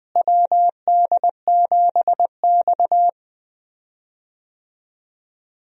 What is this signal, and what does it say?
Morse code, heard as a keyed tone, about 20 wpm, 700 Hz. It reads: WD7X
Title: USA Call Signs - 4 Characters - Repeated 2x - 20wpm